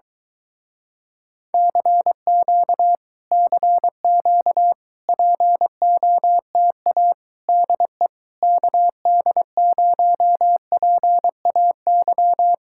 Morse code, heard as a keyed tone, about 23 wpm, 700 Hz. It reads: CQ CQ POTA DE KD0PAY